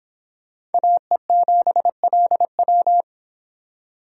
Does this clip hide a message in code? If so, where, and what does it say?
Yes: Morse code from 0.7 to 3.0 s, heard as a keyed tone, about 32 wpm, 700 Hz. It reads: AE7LW